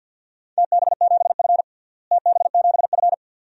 Morse code, heard as a keyed tone, about 48 wpm, 700 Hz. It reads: TB7F TB7F
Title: DXCC-100 Call Signs - 4 Characters - Repeated 2x - 50wpm